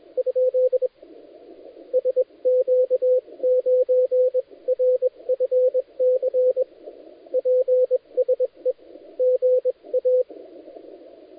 Morse code, heard as a keyed tone, about 24 wpm, spaced as for 20 wpm, 510 Hz, with radio noise. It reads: ? SQ9RFC PSE GA
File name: dlazaawansowanych / nagraniazpasma / titawaNETcontrol